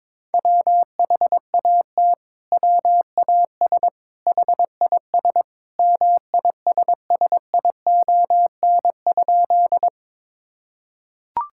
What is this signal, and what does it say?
Morse code, heard as a keyed tone, about 22 wpm, 700 Hz. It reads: WHAT WAS HIS MISSION?
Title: Sentences from Top 1600 Words 22wpm